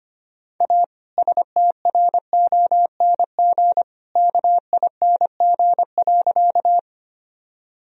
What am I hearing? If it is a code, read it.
Morse code, heard as a keyed tone, about 25 wpm, 700 Hz. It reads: A STRONG KING.
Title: Sentences from Top 1000 Words - Review Part 3 - 25wpm